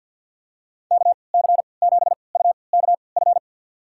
Morse code, heard as a keyed tone, about 50 wpm, spaced as for 28 wpm, 700 Hz. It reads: KC7UKF
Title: USA Call Signs - 6 Characters - 3x Character Spacing - 50wpm